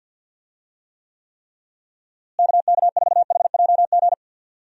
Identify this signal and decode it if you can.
Morse code, heard as a keyed tone, about 48 wpm, 700 Hz. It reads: KK4HJG
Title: USA Call Signs - 6 Characters - Repeated 3x - 50wpm